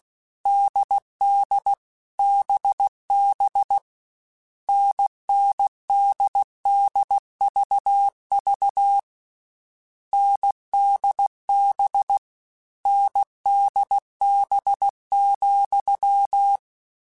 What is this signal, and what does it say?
Morse code, heard as a keyed tone, about 16 wpm, 800 Hz. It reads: DD BB NNDDVV NDB NDB,
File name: Difficult Combos / Difficult Combos-mp3s / NDB NDB6 @15